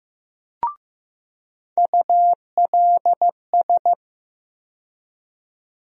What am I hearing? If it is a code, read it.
Morse code, heard as a keyed tone, about 15 wpm, 700 Hz. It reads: ULS
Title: USA Call Sign Postfixes - 3 Letters 15wpm